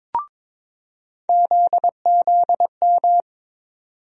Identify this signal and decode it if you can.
Morse code, heard as a keyed tone, about 22 wpm, 700 Hz. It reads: ZZM
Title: USA Call Sign Postfixes - 3 Letters 22wpm